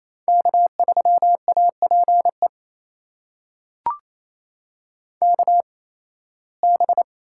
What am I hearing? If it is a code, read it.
Morse code, heard as a keyed tone, about 27 wpm, spaced as for 18 wpm, 700 Hz. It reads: K3APE KB
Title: USA Call Signs - 8x Character Spacing - 28wpm